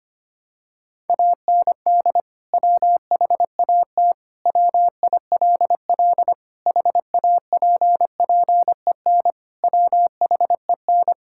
Morse code, heard as a keyed tone, about 25 wpm, 700 Hz. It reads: AND WHAT WILL HAPPEN WHEN
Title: Sentences from Top 1000 Words - Review Part 5 - 25wpm